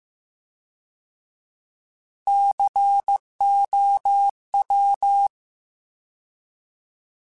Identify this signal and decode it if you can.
Morse code, heard as a keyed tone, about 15 wpm, 800 Hz. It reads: COW
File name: Alphabetical / Alphabetical mp3s / Cs @13